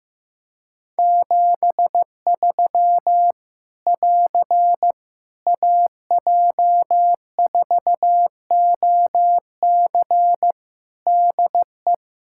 Morse code, heard as a keyed tone, about 16 wpm, 700 Hz. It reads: 73 <AR> AJ4OC DE
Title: Practice QSO - Exchange 4 - Ending - 15wpm